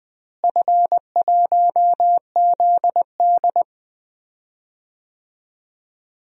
Morse code, heard as a keyed tone, about 20 wpm, 700 Hz. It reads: F1ZD
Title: DXCC-100 Call Signs - 4 Characters - Repeated 2x - 20wpm